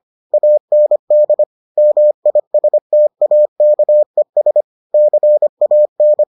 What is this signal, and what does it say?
Morse code, heard as a keyed tone, about 25 wpm, 600 Hz. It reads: AND MISTAKES CAN